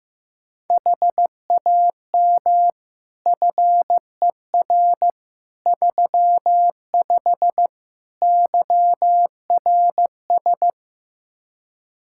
Morse code, heard as a keyed tone, about 15 wpm, 700 Hz. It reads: HAM FER 35 YRS
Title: Practice QSO - Exchange 3 - More Info - 15wpm